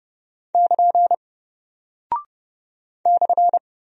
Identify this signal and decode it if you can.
Morse code, heard as a keyed tone, about 29 wpm, 700 Hz. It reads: <KN> /
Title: Single Character (Letter:Number:Punctuation:Prosign) Alt 30wpm